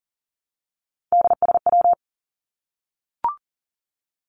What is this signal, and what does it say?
Morse code, heard as a keyed tone, about 40 wpm, 700 Hz. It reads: DSW
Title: Abbreviations 40wpm